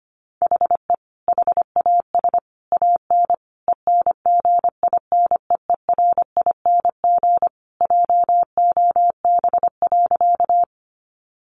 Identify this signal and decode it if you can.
Morse code, heard as a keyed tone, about 25 wpm, 700 Hz. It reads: HE HAS AN ENGINEERING JOB.
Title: Sentences from Top 1500 Words 25wpm